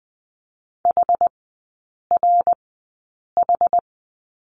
Morse code, heard as a keyed tone, about 19 wpm, spaced as for 8 wpm, 700 Hz. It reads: HRH
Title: 133 Lesson 22e - Question Mark - Three Characters 20wpm